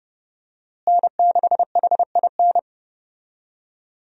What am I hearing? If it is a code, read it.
Morse code, heard as a keyed tone, about 31 wpm, 700 Hz. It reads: N6HIN
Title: USA Call Signs - 5 Characters - Repeated 2x - 30wpm